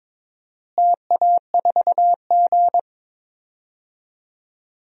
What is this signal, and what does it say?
Morse code, heard as a keyed tone, about 22 wpm, 700 Hz. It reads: TA4G
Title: DXCC-100 Call Signs - 4 Characters - Repeated 3x - 22wpm